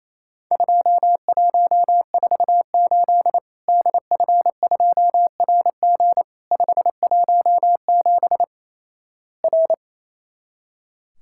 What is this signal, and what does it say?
Morse code, heard as a keyed tone, about 28 wpm, 700 Hz. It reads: 2148 DF2RG 517 R
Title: Worked All Europe DX Contest - QTC Exchange - 28wpm